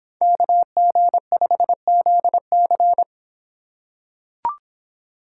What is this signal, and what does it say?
Morse code, heard as a keyed tone, about 25 wpm, 700 Hz. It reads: KG5ZC